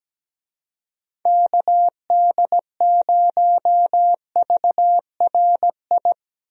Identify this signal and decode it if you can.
Morse code, heard as a keyed tone, about 17 wpm, 700 Hz. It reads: KD0VRI
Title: USA Call Signs - 6 Characters - 17wpm